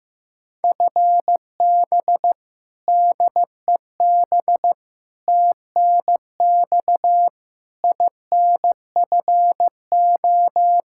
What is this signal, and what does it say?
Morse code, heard as a keyed tone, about 15 wpm, 700 Hz. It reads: FB DEB TNX INFO
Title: Practice QSO - Exchange 3 - More Info - 15wpm